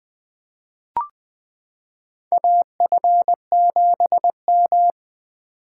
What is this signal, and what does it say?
Morse code, heard as a keyed tone, about 20 wpm, 700 Hz. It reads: AF7M
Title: 2023 SOTA Chaser Call Signs - 20wpm